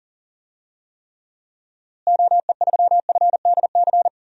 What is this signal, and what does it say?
Morse code, heard as a keyed tone, about 37 wpm, 700 Hz. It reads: OE3FDC
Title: DXCC-100 Call Signs - 6 Characters - Repeated 2x - 40wpm